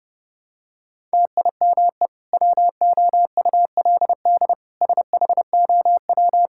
Morse code, heard as a keyed tone, about 30 wpm, 700 Hz. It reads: TIME WOULD SHOW
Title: Sentences from Top 1000 Words - Review Part 5 - 30wpm